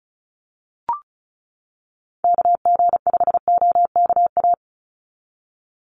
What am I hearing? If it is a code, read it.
Morse code, heard as a keyed tone, about 34 wpm, 700 Hz. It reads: KG5OKA
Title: USA Call Signs 35wpm